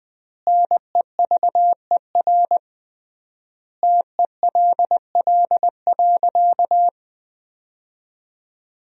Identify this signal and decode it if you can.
Morse code, heard as a keyed tone, about 21 wpm, 700 Hz. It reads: NEVER TELL.
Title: 3x Word-Spacing - Sentences from Top 300 Words - 20wpm